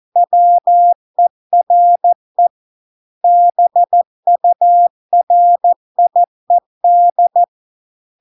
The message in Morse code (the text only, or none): WERE BURIED